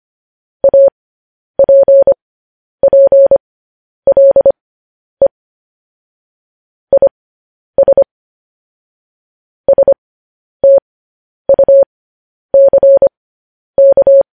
Morse code, heard as a keyed tone, about 25 wpm, spaced as for 10 wpm, 550 Hz. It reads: APPLE IS STUCK
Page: news headlines in morse code at 10 wpm